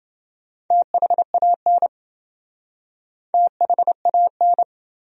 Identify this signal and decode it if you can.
Morse code, heard as a keyed tone, about 29 wpm, 700 Hz. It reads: THAN THAN